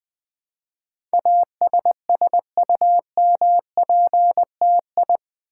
Morse code, heard as a keyed tone, about 20 wpm, 700 Hz. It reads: ASSUMPTI
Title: Build A Word - 4 Letters At A Time - 10-Letter Words - 20wpm